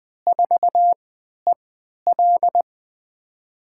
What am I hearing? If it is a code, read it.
Morse code, heard as a keyed tone, about 20 wpm, spaced as for 11 wpm, 700 Hz. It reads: 4EL
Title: USA Call Signs - 3x Character Spacing - 20wpm